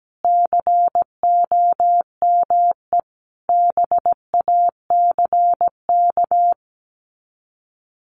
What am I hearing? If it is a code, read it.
Morse code, heard as a keyed tone, about 17 wpm, 700 Hz. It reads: COME BACK
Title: Sets of 4 Words 17wpm